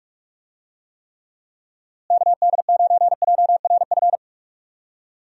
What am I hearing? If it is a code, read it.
Morse code, heard as a keyed tone, about 41 wpm, 700 Hz. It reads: KD9JRF
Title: USA Call Signs - 6 Characters - Repeated 3x - 45wpm